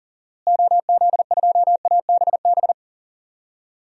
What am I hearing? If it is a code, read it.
Morse code, heard as a keyed tone, about 40 wpm, 700 Hz. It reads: OZ2ABB
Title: DXCC-100 Call Signs - 6 Characters - Repeated 3x - 40wpm